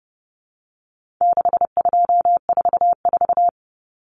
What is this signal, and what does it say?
Morse code, heard as a keyed tone, about 30 wpm, 700 Hz. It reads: 6244
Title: Numbers - 4-Digit 30wpm